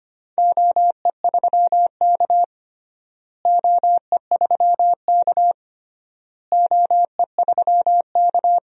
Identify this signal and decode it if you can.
Morse code, heard as a keyed tone, about 25 wpm, 700 Hz. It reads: OE3K OE3K OE3K
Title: DXCC-100 Call Signs - 4 Characters - Repeated 3x - 25wpm